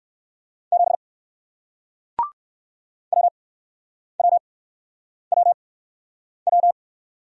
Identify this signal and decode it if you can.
Morse code, heard as a keyed tone, about 60 wpm, spaced as for 9 wpm, 700 Hz. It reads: 7 WWWW